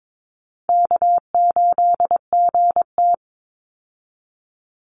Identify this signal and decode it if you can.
Morse code, heard as a keyed tone, about 22 wpm, 700 Hz. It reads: K8GT
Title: Call Signs - 22wpm